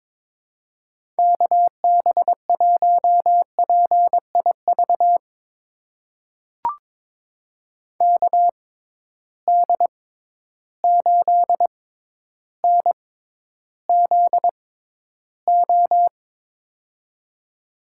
Morse code, heard as a keyed tone, about 21 wpm, spaced as for 11 wpm, 700 Hz. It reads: KB1PIV KD8NZO
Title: USA Call Signs - 6 Characters - 6x Character Spacing - 22wpm